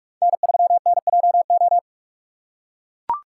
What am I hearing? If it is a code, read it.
Morse code, heard as a keyed tone, about 42 wpm, 700 Hz. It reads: N3NJO
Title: USA Call Signs - 5 Characters - 4x Character Spacing - 45wpm